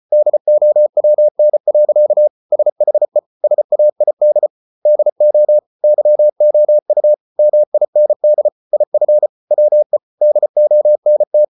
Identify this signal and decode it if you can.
Morse code, heard as a keyed tone, about 34 wpm, 600 Hz. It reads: DOWN. SHE SAID DO YOU MIND IF WE DONT